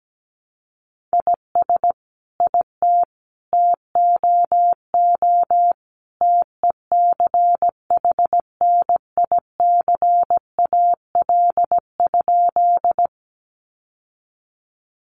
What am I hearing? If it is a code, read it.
Morse code, heard as a keyed tone, about 17 wpm, 700 Hz. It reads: IS IT TOO TECHNICAL?
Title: Sentences from Top 1100 Words 17wpm